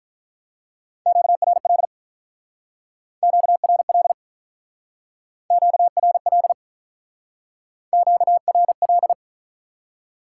Morse code, heard as a keyed tone, about 45 wpm, 700 Hz. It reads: QRL QRL QRL QRL